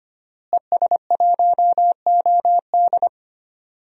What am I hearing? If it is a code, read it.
Morse code, heard as a keyed tone, about 25 wpm, 700 Hz. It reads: ES1OD
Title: DXCC-100 Call Signs - 5 Characters - Repeated 2x - 25wpm